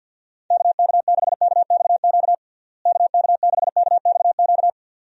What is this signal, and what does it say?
Morse code, heard as a keyed tone, about 51 wpm, 700 Hz. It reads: KK6KXQ KK6KXQ
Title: USA Call Signs - 6 Characters - Repeated 2x - 50wpm